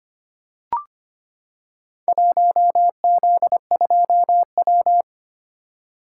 Morse code, heard as a keyed tone, about 25 wpm, 700 Hz. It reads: 1Z2W
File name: practice / 25wpm / Four Letter-Number Combinations - 25wpm